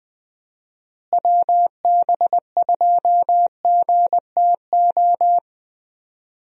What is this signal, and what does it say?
Morse code, heard as a keyed tone, about 20 wpm, 700 Hz. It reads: WB2GTO
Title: USA Call Signs - 6 Characters - Repeated 3x - 20wpm